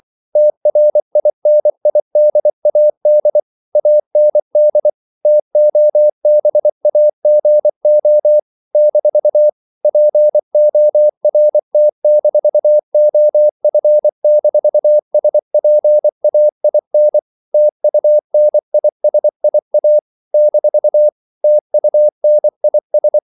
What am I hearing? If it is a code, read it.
Morse code, heard as a keyed tone, about 24 wpm, 600 Hz. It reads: TRINIDAD AND TOBAGO - PORT-OF-SPAIN TUNISIA - TUNIS
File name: CW / Practice / NationalCapitals24